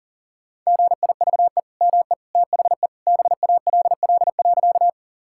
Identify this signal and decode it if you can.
Morse code, heard as a keyed tone, about 40 wpm, 690 Hz. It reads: GIVE ME THE BALL.
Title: Sentences from Top 1000 Words - Review Part 6 - 40wpm